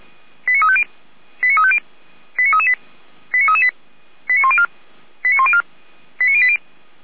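Five-tone selective call (ZVEI), heard as quick runs of five tones, tones of about 80 ms. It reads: 89350 89350 89338 89338 89104 89104 89080
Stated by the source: ZVEI five-tone call received on SDR by feuerwehr-orlamuende (fire brigade Orlamünde).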